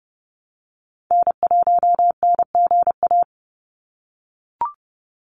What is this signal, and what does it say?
Morse code, heard as a keyed tone, about 30 wpm, 700 Hz. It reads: N1NGA
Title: USA Call Signs 30wpm